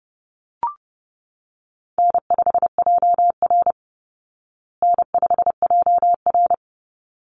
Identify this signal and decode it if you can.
Morse code, heard as a keyed tone, about 30 wpm, 700 Hz. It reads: N5JR N5JR
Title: Easy Call Signs Repeated 2x - 30wpm